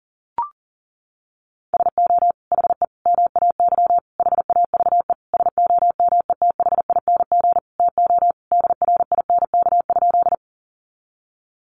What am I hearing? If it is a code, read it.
Morse code, heard as a keyed tone, about 40 wpm, 700 Hz. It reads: SO HE MAY HAVE SOMETHING TO DRINK?
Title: Sentences from Top 1500 Words 40wpm